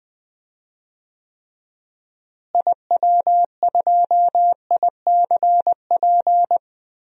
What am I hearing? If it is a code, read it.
Morse code, heard as a keyed tone, about 20 wpm, 700 Hz. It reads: IW2ICP
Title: DXCC-100 Call Signs - 6 Characters - Repeated 2x - 20wpm